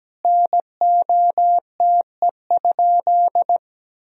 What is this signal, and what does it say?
Morse code, heard as a keyed tone, about 17 wpm, 700 Hz. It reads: NOTE?
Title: Sentences from Top 1000 Words - Review Part 2 - 17wpm